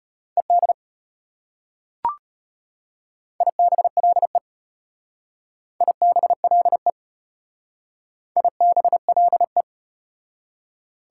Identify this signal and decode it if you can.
Morse code, heard as a keyed tone, about 34 wpm, 690 Hz. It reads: ED IBLE IBLE IBLE